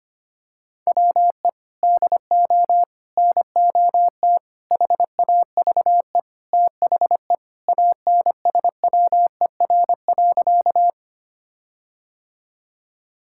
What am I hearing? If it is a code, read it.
Morse code, heard as a keyed tone, about 25 wpm, 700 Hz. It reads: WE DO NOT HAVE THE ANSWER.
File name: practice / 25wpm / Sentences with 4-Word Phrases - 25wpm